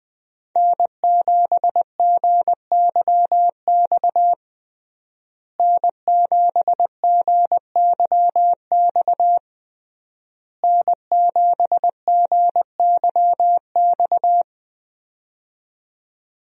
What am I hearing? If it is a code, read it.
Morse code, heard as a keyed tone, about 21 wpm, 700 Hz. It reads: N7GYX N7GYX N7GYX
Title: USA Call Signs - 5 Characters - Repeated 3x - 20wpm